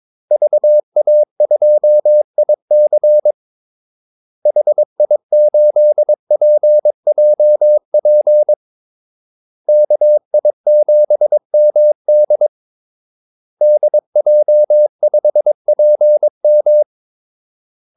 Morse code, heard as a keyed tone, about 22 wpm, 600 Hz. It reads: VA2IC HI8PJP KI7MD DJ5PM
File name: Callsigns-22wpm